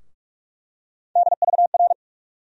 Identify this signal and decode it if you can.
Morse code, heard as a keyed tone, about 43 wpm, 700 Hz. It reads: DVR
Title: USA Call Sign Postfixes - 3 Letters 45wpm